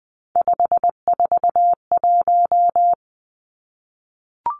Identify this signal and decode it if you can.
Morse code, heard as a keyed tone, about 20 wpm, 700 Hz. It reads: 541
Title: Three Challenging Letter-Number Combinations 20wpm